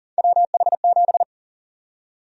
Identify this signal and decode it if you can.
Morse code, heard as a keyed tone, about 40 wpm, 700 Hz. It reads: WH7
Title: USA Call Sign Prefixes - 40wpm